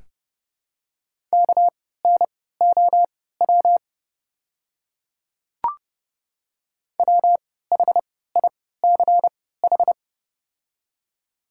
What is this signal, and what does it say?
Morse code, heard as a keyed tone, about 30 wpm, spaced as for 17 wpm, 690 Hz. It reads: KNOW WHICH
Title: 3x Character Spacing - Top 100 Words - 30wpm